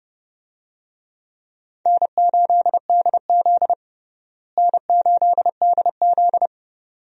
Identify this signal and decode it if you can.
Morse code, heard as a keyed tone, about 32 wpm, 700 Hz. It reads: N8DZ N8DZ
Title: USA Call Signs - 4 Characters - Repeated 2x - 30wpm